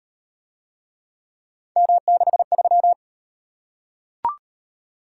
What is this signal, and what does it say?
Morse code, heard as a keyed tone, about 35 wpm, 700 Hz. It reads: M63